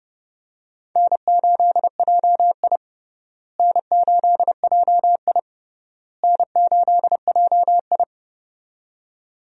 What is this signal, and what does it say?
Morse code, heard as a keyed tone, about 31 wpm, 700 Hz. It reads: N8JI N8JI N8JI
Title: USA Call Signs - 4 Characters - Repeated 3x - 30wpm